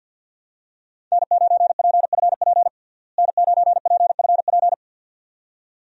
Morse code, heard as a keyed tone, about 51 wpm, 700 Hz. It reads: N9PFP N9PFP
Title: USA Call Signs - 5 Characters - Repeated 2x - 50wpm